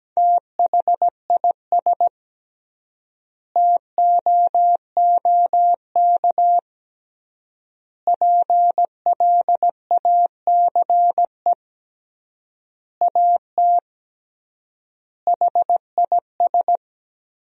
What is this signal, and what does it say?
Morse code, heard as a keyed tone, about 17 wpm, 700 Hz. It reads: THIS TOOK PLACE AT HIS
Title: 3x Word-Spacing - Sentences from Top 300 Words - 17wpm